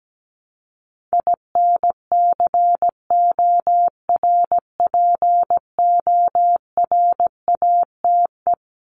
Morse code, heard as a keyed tone, about 17 wpm, 700 Hz. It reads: INCORPORATE
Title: Top 1751-2000 Words 17wpm